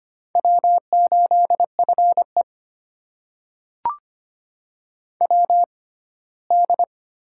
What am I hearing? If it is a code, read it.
Morse code, heard as a keyed tone, about 25 wpm, spaced as for 17 wpm, 700 Hz. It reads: W8FE WD
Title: USA Call Signs - 4 Characters - 6x Character Spacing - 25wpm